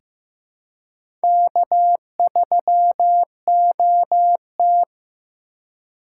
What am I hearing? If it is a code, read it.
Morse code, heard as a keyed tone, about 15 wpm, 700 Hz. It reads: K3OT
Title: USA Call Signs - 4 Characters - 2x Character Spacing - 15wpm